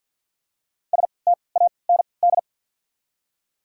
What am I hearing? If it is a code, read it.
Morse code, heard as a keyed tone, about 48 wpm, spaced as for 28 wpm, 690 Hz. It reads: STAND
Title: 3x Character Spacing - Top 300 Words - 50wpm